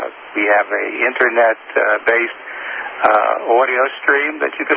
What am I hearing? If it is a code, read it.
Emergency Alert System (SAME) tones, heard as harsh buzzing bursts, with speech. It is heard from 0.9 to 4.3 s.